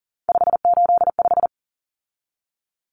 Morse code, heard as a keyed tone, about 40 wpm, 700 Hz. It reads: 585
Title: Common Cut Numbers - 3-Digit 40wpm